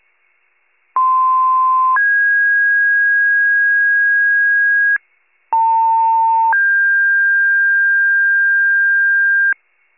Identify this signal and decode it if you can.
Two-tone page, heard as a one-second tone, then a three-second tone, about 1040 then 1650 Hz, about 900 then 1600 Hz. It is heard from 1.0 to 9.5 s.
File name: pager / 78 RESCUE DUTY / wcalert